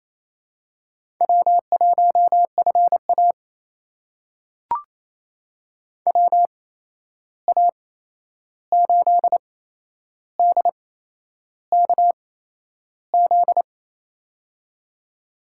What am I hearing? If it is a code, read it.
Morse code, heard as a keyed tone, about 28 wpm, spaced as for 11 wpm, 700 Hz. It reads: W1FA WA8DKZ
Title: USA Call Signs - 8x Character Spacing - 28wpm